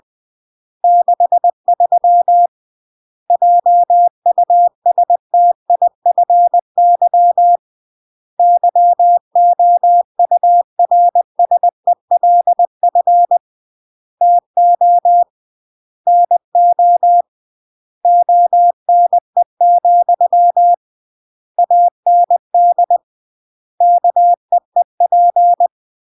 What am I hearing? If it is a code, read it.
Morse code, heard as a keyed tone, about 20 wpm, 700 Hz. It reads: 63 JUSTIFY YOURSELF TO NO ONE, AND KEEP
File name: CW / jodorowsky / 82 commandments 20-20